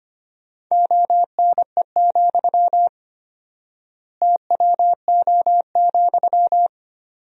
Morse code, heard as a keyed tone, about 25 wpm, 700 Hz. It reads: ONE, TWO,
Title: 4x Word-Spacing - Sentences from Top 200 Words - 25wpm